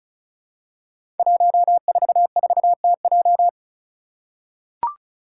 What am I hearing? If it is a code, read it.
Morse code, heard as a keyed tone, about 36 wpm, 700 Hz. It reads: 144TJ
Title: Five Challenging Letter-Number Combinations 35wpm